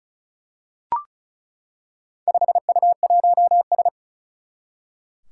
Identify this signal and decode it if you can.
Morse code, heard as a keyed tone, about 36 wpm, 700 Hz. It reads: 5U1S
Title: Four Challenging Letter-Number Combinations 35wpm